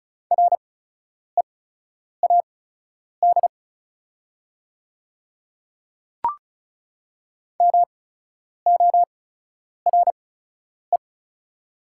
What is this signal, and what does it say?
Morse code, heard as a keyed tone, about 36 wpm, spaced as for 10 wpm, 700 Hz. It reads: READ MORE
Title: 8x Character Spacing - Top 300 Words - 35wpm